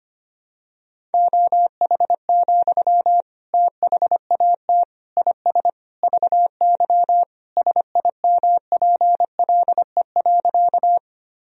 Morse code, heard as a keyed tone, about 25 wpm, 700 Hz. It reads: OH, THAT IS VY SIMPLE.
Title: Sentences from Top 500 Words and Abbreviations 25wpm